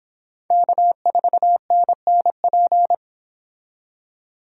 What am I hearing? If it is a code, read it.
Morse code, heard as a keyed tone, about 25 wpm, 700 Hz. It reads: K4NNP